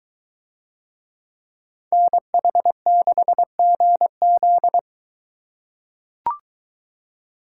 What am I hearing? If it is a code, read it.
Morse code, heard as a keyed tone, about 20 wpm, 700 Hz. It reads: NH6GZ